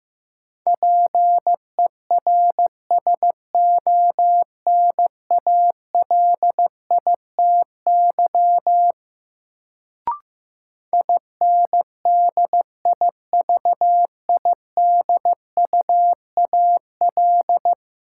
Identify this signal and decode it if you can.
Morse code, heard as a keyed tone, about 15 wpm, 700 Hz. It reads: PERSONALITY INDIVIDUAL
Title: Sentences from Top 1400 Words - New Word Drill Alt - 15wpm